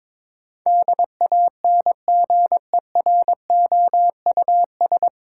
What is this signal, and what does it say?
Morse code, heard as a keyed tone, about 22 wpm, 700 Hz. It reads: DANGEROUS
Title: Sentences from Top 1200 Words - New Word Drill - 22wpm